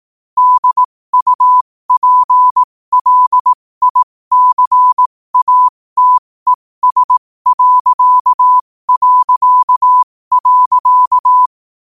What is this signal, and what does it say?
Morse code, heard as a keyed tone, about 18 wpm, spaced as for 15 wpm, 1000 Hz. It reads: DUPLICATES...
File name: qotdfiles / 15wpm